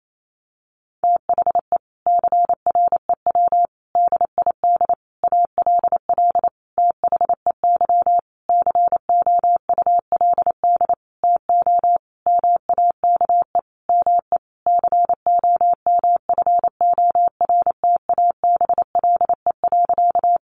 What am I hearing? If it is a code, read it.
Morse code, heard as a keyed tone, about 28 wpm, 700 Hz. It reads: THE CREW DID ALL THEY COULD TO MAKE ME COMFORTABLE.
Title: Sentences from Top 1500 Words 28wpm